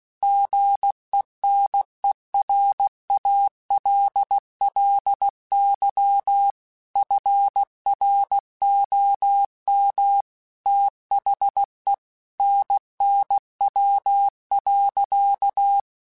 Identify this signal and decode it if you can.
Morse code, heard as a keyed tone, about 16 wpm, 800 Hz. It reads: GENERALLY FROM THE NNW.
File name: Weather Sets / Weather Sets-mp3s / Weather 5 12 @15